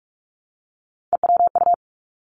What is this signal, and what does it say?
Morse code, heard as a keyed tone, about 46 wpm, 700 Hz. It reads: EWU
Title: Three Challenging Letter-Number Combinations 45wpm